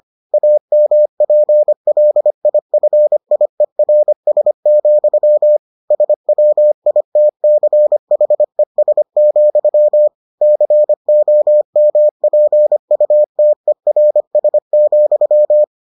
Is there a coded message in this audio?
Yes: Morse code from 0.3 to 15.6 s, heard as a keyed tone, about 25 wpm, 600 Hz. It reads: AMPLIFIERS, SWITCHES, COMPUTERS,